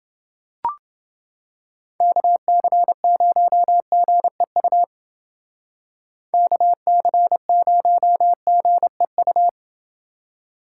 Morse code, heard as a keyed tone, about 29 wpm, 700 Hz. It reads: KC0GEU KC0GEU